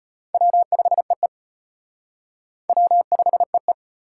Morse code, heard as a keyed tone, about 38 wpm, 700 Hz. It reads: W5EE W5EE